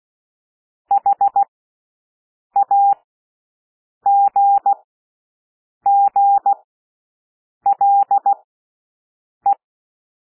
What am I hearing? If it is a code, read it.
Morse code, heard as a keyed tone, about 16 wpm, spaced as for 6 wpm, 800 Hz. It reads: HAGGLE
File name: cwmp3 / W1 SOUND WORDS 16 wpm